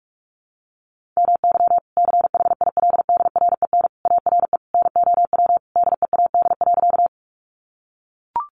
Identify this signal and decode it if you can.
Morse code, heard as a keyed tone, about 44 wpm, 700 Hz. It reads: MY CHILDREN ARE NOW DEAD.
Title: Sentences from Top 1500 Words 45wpm